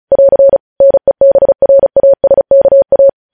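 Morse code, heard as a keyed tone, about 36 wpm, 550 Hz. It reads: <AR> NEBRASKA
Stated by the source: Morse code news headlines sent at 35, beginning with vvv.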